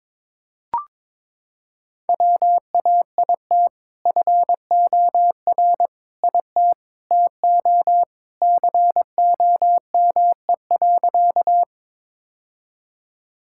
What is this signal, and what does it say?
Morse code, heard as a keyed tone, about 22 wpm, 700 Hz. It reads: WAIT FOR IT TO COME.
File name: practice / 22wpm / Sentences with 2-Word Phrases - 22wpm